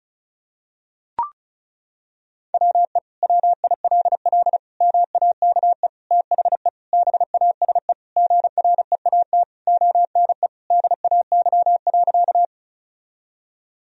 Morse code, heard as a keyed tone, about 36 wpm, 700 Hz. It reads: WE WILL MAKE THE BASE GREAT ONE DAY.
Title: Sentences from Top 1000 Words - Review Part 1 - 35wpm